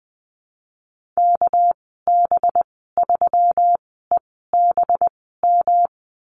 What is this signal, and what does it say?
Morse code, heard as a keyed tone, about 20 wpm, spaced as for 15 wpm, 700 Hz. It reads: KB3EBM
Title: 231 Lesson 37g - Letter x - Call Signs 20wpm